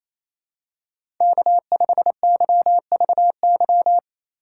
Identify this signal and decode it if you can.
Morse code, heard as a keyed tone, about 28 wpm, 700 Hz. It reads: K5YVY
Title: K1USN SST QSO Practice - 28wpm